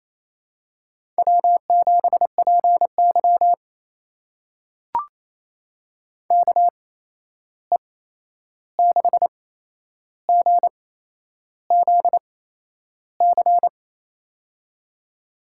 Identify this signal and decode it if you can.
Morse code, heard as a keyed tone, about 27 wpm, spaced as for 10 wpm, 700 Hz. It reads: W7PY KE6GZC
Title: USA Call Signs - 8x Character Spacing - 28wpm